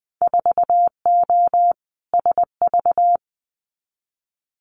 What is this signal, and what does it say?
Morse code, heard as a keyed tone, about 20 wpm, 700 Hz. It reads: 4O SV
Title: Winter Field Day Exchanges Encore 20wpm